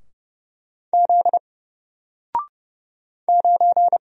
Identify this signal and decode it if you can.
Morse code, heard as a keyed tone, about 31 wpm, 700 Hz. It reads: Z 9
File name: Single-Letter-Number-30wpm